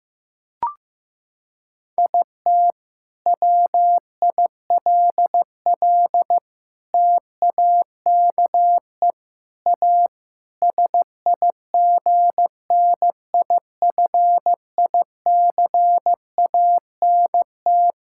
Morse code, heard as a keyed tone, about 15 wpm, 700 Hz. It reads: IT WILL TAKE A SIGNIFICANT
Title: Sentences from Top 1000 Words - Review Part 6 - 15wpm